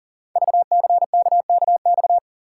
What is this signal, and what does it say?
Morse code, heard as a keyed tone, about 42 wpm, 700 Hz. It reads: VCKKX
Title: Code Groups - 5 Letters - 40wpm